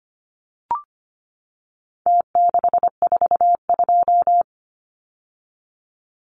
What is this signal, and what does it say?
Morse code, heard as a keyed tone, about 25 wpm, 700 Hz. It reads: T642
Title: Numbers - Contest Style - 4-Digit 25wpm